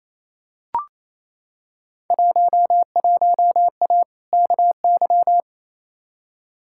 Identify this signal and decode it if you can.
Morse code, heard as a keyed tone, about 28 wpm, 700 Hz. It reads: 11A KY
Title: Field Day Exchanges Encore - 28wpm